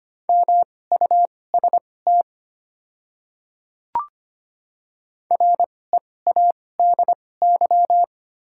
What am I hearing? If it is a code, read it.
Morse code, heard as a keyed tone, about 25 wpm, spaced as for 18 wpm, 700 Hz. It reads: MUST READY
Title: 2x Character Spacing - Top 500 Words - 25wpm